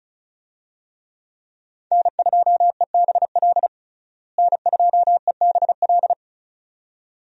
Morse code, heard as a keyed tone, about 34 wpm, 700 Hz. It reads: N2EBL N2EBL
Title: USA Call Signs - 5 Characters - Repeated 2x - 35wpm